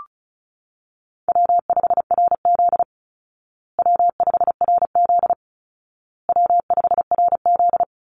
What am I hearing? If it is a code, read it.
Morse code, heard as a keyed tone, about 34 wpm, 700 Hz. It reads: W5RZ W5RZ W5RZ
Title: Easy Call Signs Repeated 3x - 35wpm